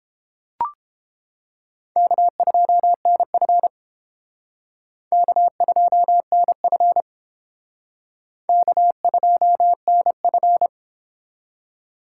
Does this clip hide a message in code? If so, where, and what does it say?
Yes: Morse code from 2.0 to 10.7 s, heard as a keyed tone, about 29 wpm, 700 Hz. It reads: K2NF K2NF K2NF